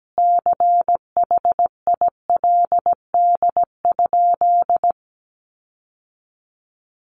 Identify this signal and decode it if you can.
Morse code, heard as a keyed tone, about 17 wpm, 700 Hz. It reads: CHILD?
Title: Sentences from Top 800 Words 17wpm